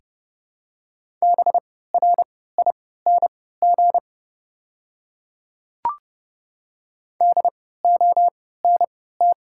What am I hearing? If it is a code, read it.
Morse code, heard as a keyed tone, about 30 wpm, spaced as for 17 wpm, 700 Hz. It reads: BRING DONT
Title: 3x Character Spacing - Top 500 Words - 30wpm